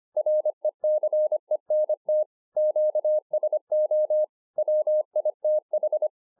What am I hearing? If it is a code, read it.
Morse code, heard as a keyed tone, about 25 wpm, 610 Hz. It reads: RECENT QSO WITH